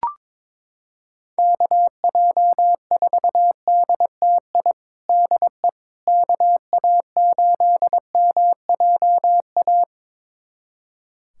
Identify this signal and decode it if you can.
Morse code, heard as a keyed tone, about 22 wpm, 700 Hz. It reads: KJ4DTI DE KA8MJA
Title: USA Callsign DE Callsign 22wpm